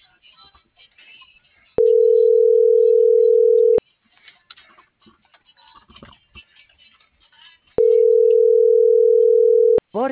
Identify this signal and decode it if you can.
Ringback tone (North America), 440 + 480 Hz.